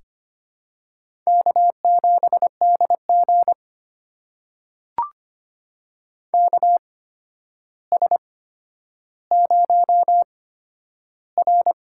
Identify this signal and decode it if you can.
Morse code, heard as a keyed tone, about 25 wpm, spaced as for 11 wpm, 700 Hz. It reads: K7DG KS0R
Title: USA Call Signs - 4 Characters - 8x Character Spacing - 25wpm